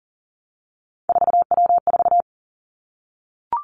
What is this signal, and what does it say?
Morse code, heard as a keyed tone, about 40 wpm, 700 Hz. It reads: 4W4
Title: Three Challenging Letter-Number Combinations 40wpm